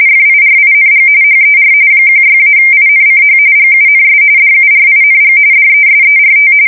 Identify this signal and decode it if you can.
RTTY, about 45.45 baud. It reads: FIRING HER 16-INCH GUNS DURING DESERT
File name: fsk170 / USS-Missouri